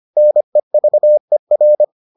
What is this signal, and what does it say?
Morse code, heard as a keyed tone, about 25 wpm, 600 Hz. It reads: NEVER